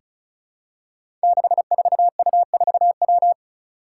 Morse code, heard as a keyed tone, about 36 wpm, 700 Hz. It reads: 64U4W
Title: Five Challenging Letter-Number Combinations 35wpm